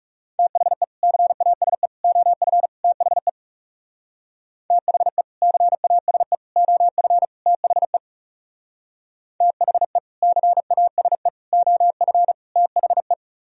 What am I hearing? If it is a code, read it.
Morse code, heard as a keyed tone, about 35 wpm, 700 Hz. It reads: THE CASE OF THE THE CASE OF THE THE CASE OF THE
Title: Speed-Racing - Sets of 4 Words 30wpm